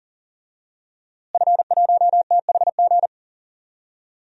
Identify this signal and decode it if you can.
Morse code, heard as a keyed tone, about 40 wpm, 700 Hz. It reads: F1THG
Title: DXCC-100 Call Signs - 5 Characters - Repeated 3x - 40wpm